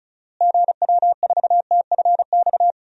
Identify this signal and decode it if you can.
Morse code, heard as a keyed tone, about 34 wpm, 700 Hz. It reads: GW4TFX
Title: DXCC-100 Call Signs - 6 Characters - Repeated 2x - 35wpm